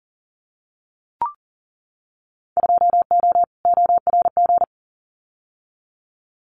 Morse code, heard as a keyed tone, about 40 wpm, 700 Hz. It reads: 2O ORG
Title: Winter Field Day Exchanges 40wpm